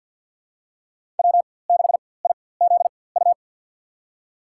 Morse code, heard as a keyed tone, about 47 wpm, spaced as for 23 wpm, 700 Hz. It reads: W6IZU
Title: USA Call Signs - 4x Character Spacing - 50wpm